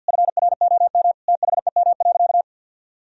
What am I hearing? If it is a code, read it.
Morse code, heard as a keyed tone, about 53 wpm, 700 Hz. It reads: FROM THEM.
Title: Sentences from Top 1000 Words - Review Part 4 - 50wpm